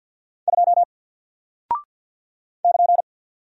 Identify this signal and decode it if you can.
Morse code, heard as a keyed tone, about 48 wpm, 700 Hz. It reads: 2 <KN>
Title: Single Character (Letter:Number:Punctuation:Prosign) Alt 50wpm